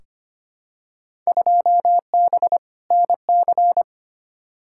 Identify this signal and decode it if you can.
Morse code, heard as a keyed tone, about 25 wpm, 700 Hz. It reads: 2B NC
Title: Field Day Exchanges Encore - 25wpm